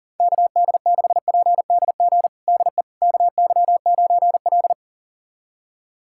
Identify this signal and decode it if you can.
Morse code, heard as a keyed tone, about 40 wpm, 700 Hz. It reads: KD6PDG DE KY9L
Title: USA Callsign DE Callsign 40wpm